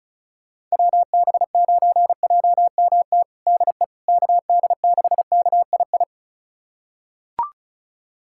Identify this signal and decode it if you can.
Morse code, heard as a keyed tone, about 36 wpm, 700 Hz. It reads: WB9JMT DE KD6KII
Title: USA Callsign DE Callsign - 6 Characters - 35wpm